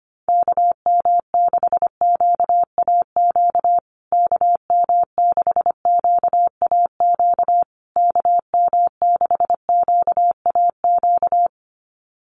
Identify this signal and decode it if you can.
Morse code, heard as a keyed tone, about 25 wpm, 700 Hz. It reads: KM6QAQ KM6QAQ KM6QAQ
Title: USA Call Signs Repeated 3x 25wpm